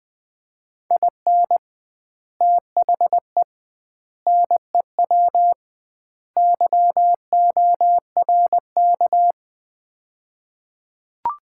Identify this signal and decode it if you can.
Morse code, heard as a keyed tone, about 20 wpm, 700 Hz. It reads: IN THE NEW YORK
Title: 2x Word-Spacing - Sets of 4 Words - 20wpm